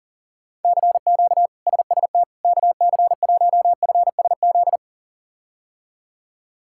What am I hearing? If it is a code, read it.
Morse code, heard as a keyed tone, about 40 wpm, 700 Hz. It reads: CQ SST KC1FSZ
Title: K1USN SST QSO Practice - 40wpm